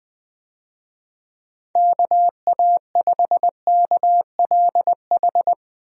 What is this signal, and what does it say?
Morse code, heard as a keyed tone, about 20 wpm, 700 Hz. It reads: KA5KLH
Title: USA Call Signs - 6 Characters - 20wpm